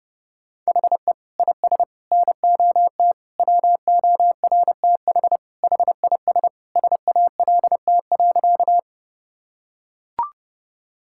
Morse code, heard as a keyed tone, about 30 wpm, 700 Hz. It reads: HE IS NOT WORTH HIS SALT.
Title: Sentences from Top 1000 Words - Review Part 5 - 30wpm